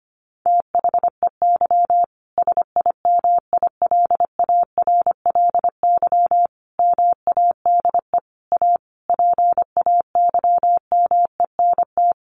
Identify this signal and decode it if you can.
Morse code, heard as a keyed tone, about 25 wpm, 700 Hz. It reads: THEY SIMILARLY MADE A PAYMENT
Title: Sentences from Top 1400 Words 25wpm